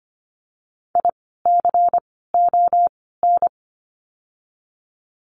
Morse code, heard as a keyed tone, about 25 wpm, 700 Hz. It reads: I C O N
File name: course / 25:10wpm / 094 Lesson 16f - Letter c - Words 25wpm